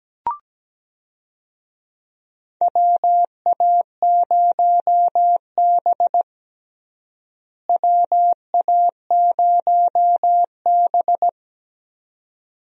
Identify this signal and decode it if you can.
Morse code, heard as a keyed tone, about 17 wpm, 700 Hz. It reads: WA0B WA0B
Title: USA Call Signs - 4 Characters - Repeated 3x - 17wpm